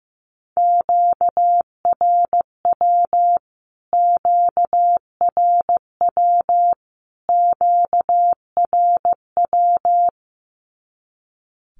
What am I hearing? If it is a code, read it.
Morse code, heard as a keyed tone, about 15 wpm, 700 Hz. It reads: QRW QRW QRW
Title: Q-Codes Repeated 3x 15wpm